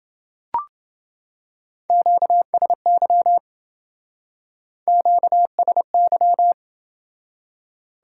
Morse code, heard as a keyed tone, about 27 wpm, 700 Hz. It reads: QSY QSY